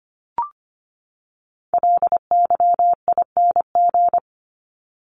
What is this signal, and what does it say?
Morse code, heard as a keyed tone, about 25 wpm, 700 Hz. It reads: LYING
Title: ING Suffix - Top 100 Words with Base Words 25wpm